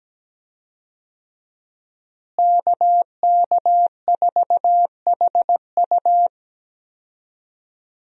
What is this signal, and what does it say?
Morse code, heard as a keyed tone, about 17 wpm, 700 Hz. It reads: KK4HU
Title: USA Call Signs - 5 Characters - Repeated 3x - 17wpm